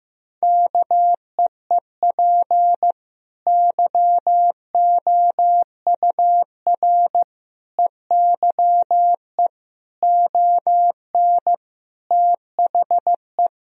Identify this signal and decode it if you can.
Morse code, heard as a keyed tone, about 15 wpm, 700 Hz. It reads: KEEP YOUR EYE ON THE
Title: Sentences from Top 1000 Words - Review Part 4 - 15wpm